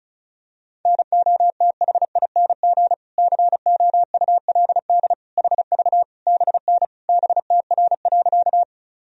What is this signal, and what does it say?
Morse code, heard as a keyed tone, about 36 wpm, 700 Hz. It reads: NOTHING COULD HV BN BTR.
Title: Sentences from Top 500 Words and Abbreviations 35wpm